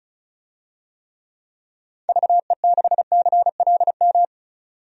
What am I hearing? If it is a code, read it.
Morse code, heard as a keyed tone, about 34 wpm, 700 Hz. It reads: VE6CLM
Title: DXCC-100 Call Signs - 6 Characters - 35wpm